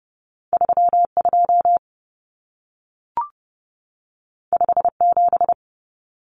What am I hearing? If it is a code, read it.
Morse code, heard as a keyed tone, about 30 wpm, 700 Hz. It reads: 32 57
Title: Common Cut Numbers - 2-Digit 30wpm